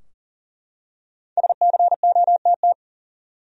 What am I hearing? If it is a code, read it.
Morse code, heard as a keyed tone, about 37 wpm, 700 Hz. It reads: SCOTT
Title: USA Common Names - Max 5 Letters - 40wpm